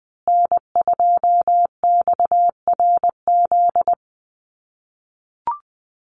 Morse code, heard as a keyed tone, about 20 wpm, 700 Hz. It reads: N2XRZ
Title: USA Call Signs Repeated 3x 20wpm